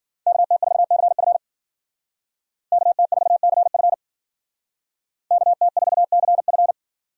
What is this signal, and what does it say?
Morse code, heard as a keyed tone, about 50 wpm, 700 Hz. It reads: KT4CF KT4CF KT4CF